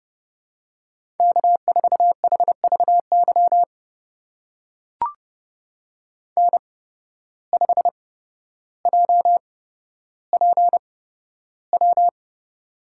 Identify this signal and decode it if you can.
Morse code, heard as a keyed tone, about 30 wpm, spaced as for 13 wpm, 700 Hz. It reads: K4HVY N5JPW
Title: USA Call Signs - 5 Characters - 8x Character Spacing - 30wpm